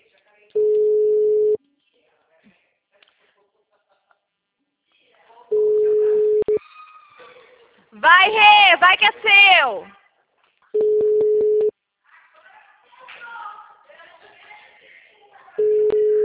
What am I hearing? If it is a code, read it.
Ringback tone (Europe, 425 Hz), with speech.